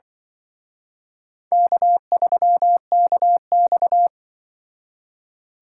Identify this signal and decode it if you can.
Morse code, heard as a keyed tone, about 25 wpm, 700 Hz. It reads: K3KX
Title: Speed Racing - USA 1x2 Call Signs 20wpm